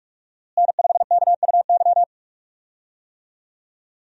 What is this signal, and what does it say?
Morse code, heard as a keyed tone, about 46 wpm, 700 Hz. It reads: N5KUY
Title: USA Call Signs - 5 Characters - Repeated 3x - 45wpm